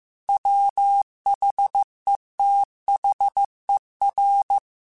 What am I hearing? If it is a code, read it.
Morse code, heard as a keyed tone, about 15 wpm, 800 Hz. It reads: WHETHER
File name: Chuckles / Chuckles-mp3s / Glass Half Full @13